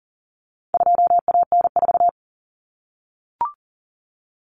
Morse code, heard as a keyed tone, about 40 wpm, 700 Hz. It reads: 2AN4
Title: Common Cut Numbers - 4-Digit 40wpm